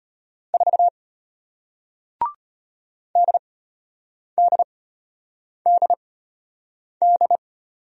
Morse code, heard as a keyed tone, about 33 wpm, spaced as for 8 wpm, 700 Hz. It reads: 4 DDDD